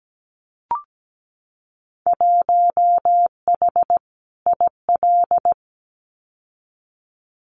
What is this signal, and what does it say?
Morse code, heard as a keyed tone, about 17 wpm, 700 Hz. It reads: 1H IL
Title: Winter Field Day Exchanges 17wpm